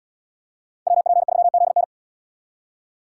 Morse code, heard as a keyed tone, about 75 wpm, 700 Hz. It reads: WP3ZN